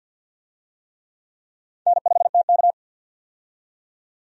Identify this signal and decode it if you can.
Morse code, heard as a keyed tone, about 49 wpm, 700 Hz. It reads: N5TK
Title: USA Call Signs - 4 Characters - 50wpm